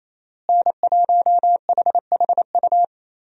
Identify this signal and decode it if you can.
Morse code, heard as a keyed tone, about 28 wpm, 700 Hz. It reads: N1HHU
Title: USA Call Signs - 5 Characters - Repeated 3x - 28wpm